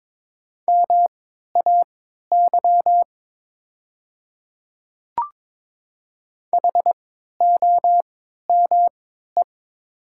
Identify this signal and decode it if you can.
Morse code, heard as a keyed tone, about 22 wpm, spaced as for 12 wpm, 700 Hz. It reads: MAY HOME
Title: 3x Character Spacing - Top 200 Words - 22wpm